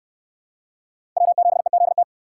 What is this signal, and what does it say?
Morse code, heard as a keyed tone, about 67 wpm, 700 Hz. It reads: W7EZT